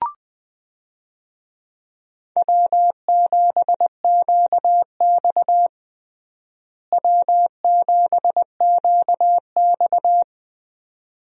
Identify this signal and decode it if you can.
Morse code, heard as a keyed tone, about 21 wpm, 700 Hz. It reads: W7QX W7QX
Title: USA Call Signs - 4 Characters - Repeated 2x - 20wpm